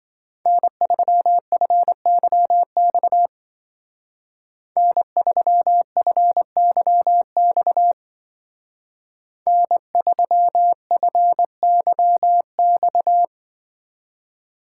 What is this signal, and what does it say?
Morse code, heard as a keyed tone, about 25 wpm, 700 Hz. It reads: N3FYX N3FYX N3FYX
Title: Speed Racing - USA 1x3 Call Signs 20wpm